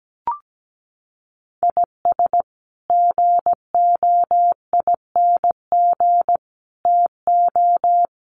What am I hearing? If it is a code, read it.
Morse code, heard as a keyed tone, about 17 wpm, 700 Hz. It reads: IS GOING TO
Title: Sets of 5 Words 17wpm